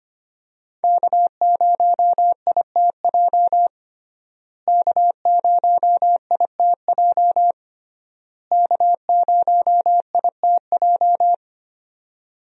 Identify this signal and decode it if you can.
Morse code, heard as a keyed tone, about 25 wpm, 700 Hz. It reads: K0ITJ K0ITJ K0ITJ
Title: USA Call Signs - 5 Characters - Repeated 3x - 25wpm